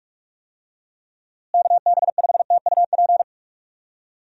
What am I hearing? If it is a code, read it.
Morse code, heard as a keyed tone, about 46 wpm, 700 Hz. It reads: KB5TUP
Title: USA Call Signs - 6 Characters - Repeated 2x - 45wpm